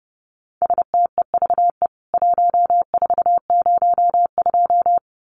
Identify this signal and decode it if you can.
Morse code, heard as a keyed tone, about 30 wpm, 700 Hz. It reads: STEVE 1402
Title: CWT Exchange 30wpm